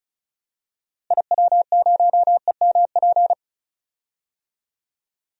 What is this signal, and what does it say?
Morse code, heard as a keyed tone, about 36 wpm, 700 Hz. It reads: IW0EMP
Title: DXCC-100 Call Signs - 6 Characters - Repeated 3x - 35wpm